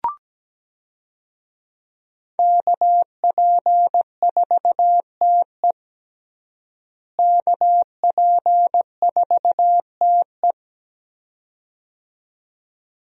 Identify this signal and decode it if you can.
Morse code, heard as a keyed tone, about 17 wpm, 700 Hz. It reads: KP4TE KP4TE
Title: DXCC-100 Call Signs - 5 Characters - Repeated 2x - 17wpm